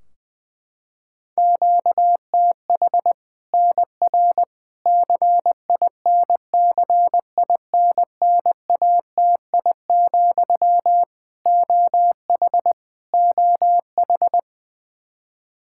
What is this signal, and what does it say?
Morse code, heard as a keyed tone, about 20 wpm, 700 Hz. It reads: QTH NR CINCINNATI, OH OH
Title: Practice QSO - Exchange 1 - Beginning - 20wpm